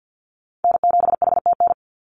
Morse code, heard as a keyed tone, about 50 wpm, 700 Hz. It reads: N7HTN
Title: USA Call Signs 50wpm